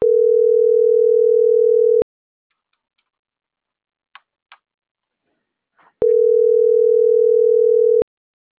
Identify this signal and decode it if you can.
Ringback tone (North America), 440 + 480 Hz.
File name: CENTRAL MABE / out maria - .64520